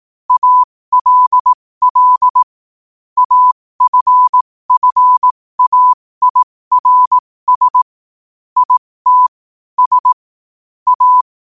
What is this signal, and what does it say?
Morse code, heard as a keyed tone, about 18 wpm, spaced as for 15 wpm, 1000 Hz. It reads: ALL AFFAIRS IT S A
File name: qotdfiles / 15wpm